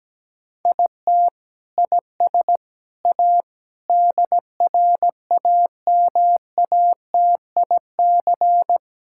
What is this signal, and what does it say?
Morse code, heard as a keyed tone, about 17 wpm, 700 Hz. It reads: IT IS A DRAMATIC
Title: Sentences from Top 1600 Words 17wpm